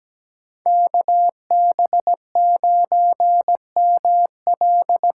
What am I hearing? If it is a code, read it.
Morse code, heard as a keyed tone, about 17 wpm, 700 Hz. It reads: KB9ML
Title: USA Call Signs - 6 Characters - 6x Character Spacing - 17wpm